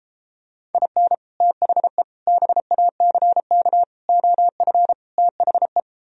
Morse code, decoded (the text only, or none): IN THE BACK OF THE